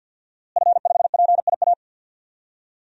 Morse code, heard as a keyed tone, about 52 wpm, 700 Hz. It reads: F5PIA
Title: DXCC-100 Call Signs - 5 Characters - Repeated 3x - 50wpm